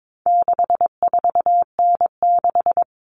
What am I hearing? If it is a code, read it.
Morse code, heard as a keyed tone, about 22 wpm, 700 Hz. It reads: 64N6
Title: All Cut Numbers - 4-Digit 22wpm